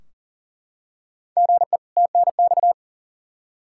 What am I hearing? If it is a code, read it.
Morse code, heard as a keyed tone, about 40 wpm, 700 Hz. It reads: GE TNX
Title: K1USN SST QSO Practice 40wpm